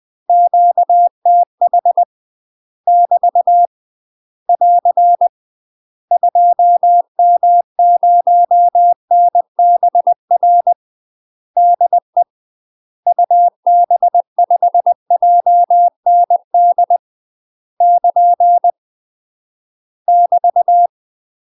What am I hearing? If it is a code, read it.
Morse code, heard as a keyed tone, about 20 wpm, 700 Hz. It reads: QTH <BT> <AR> 2M0NBR DE UB5JND <KN> <BT>
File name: UK MASTER QSOs TRAINING / 20WPM / UK MASTER QSOs 0051